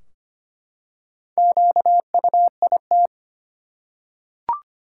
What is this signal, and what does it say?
Morse code, heard as a keyed tone, about 25 wpm, 700 Hz. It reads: QUIT
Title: Four Letter Words - 25wpm